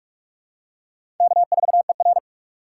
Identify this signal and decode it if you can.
Morse code, heard as a keyed tone, about 47 wpm, 700 Hz. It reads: K4ER
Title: USA Call Signs - 4 Characters - 4x Character Spacing - 45wpm